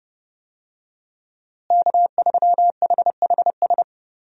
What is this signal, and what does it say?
Morse code, heard as a keyed tone, about 32 wpm, 700 Hz. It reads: K3HHS
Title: USA Call Signs - 5 Characters - Repeated 3x - 30wpm